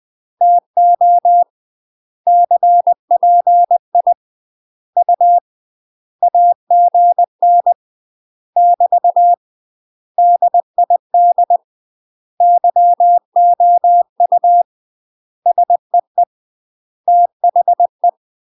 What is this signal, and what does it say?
Morse code, heard as a keyed tone, about 20 wpm, 700 Hz. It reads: TO CPI U AGN <BT> DID YOU SEE THE
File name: UK MASTER QSOs TRAINING/20WPM